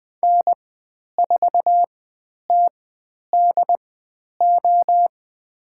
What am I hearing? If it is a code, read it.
Morse code, heard as a keyed tone, about 20 wpm, spaced as for 10 wpm, 700 Hz. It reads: N4TDO
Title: 109 Lesson 18g - Letter m - Call Signs 20wpm